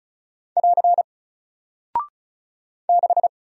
Morse code, heard as a keyed tone, about 36 wpm, 700 Hz. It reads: <AR> 6
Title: Single Character (Letter:Number:Punctuation:Prosign) Alt 35wpm